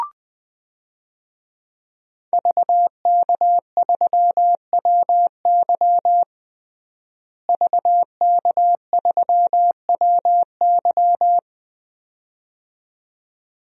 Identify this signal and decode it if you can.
Morse code, heard as a keyed tone, about 20 wpm, 700 Hz. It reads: VK3WY VK3WY
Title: DXCC-100 Call Signs - 5 Characters - Repeated 2x - 20wpm